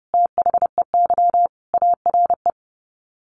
Morse code, heard as a keyed tone, about 30 wpm, 700 Hz. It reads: THEY ARE
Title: Sets of 5 Words Encore2 30wpm